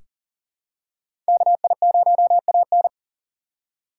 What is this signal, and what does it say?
Morse code, heard as a keyed tone, about 40 wpm, 700 Hz. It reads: KI0AN
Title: USA Call Signs - 5 Characters - 2x Character Spacing - 40wpm